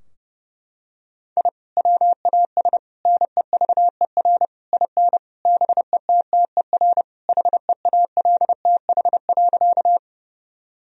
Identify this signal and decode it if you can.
Morse code, heard as a keyed tone, about 32 wpm, 700 Hz. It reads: I WAS NEVER IN BETTER HEALTH.
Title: Sentences from Top 1000 Words - Review Part 3 - 30wpm